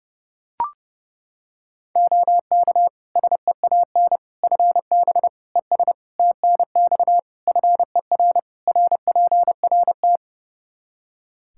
Morse code, heard as a keyed tone, about 30 wpm, 700 Hz. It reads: OK SEAN FB ES TNX FER RPRT
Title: Practice QSO - Exchange 2 - Info - 30wpm